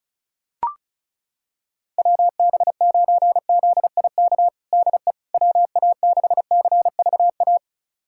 Morse code, heard as a keyed tone, about 34 wpm, 700 Hz. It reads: WB9ZIK DE WA6CVA
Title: USA Callsign DE Callsign - 6 Characters - 35wpm